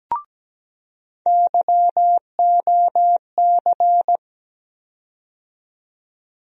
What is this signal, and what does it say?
Morse code, heard as a keyed tone, about 17 wpm, 700 Hz. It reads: YOC
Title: USA Call Sign Postfixes - 3 Letters 17wpm